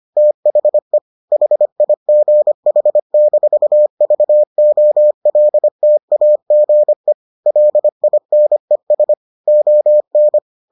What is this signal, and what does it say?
Morse code, heard as a keyed tone, about 25 wpm, 600 Hz. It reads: THE HIGH-VOLTAGE LINES ON